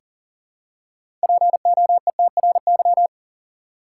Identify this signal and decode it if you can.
Morse code, heard as a keyed tone, about 40 wpm, 700 Hz. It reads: POETRY
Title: Six Letter Words - 40wpm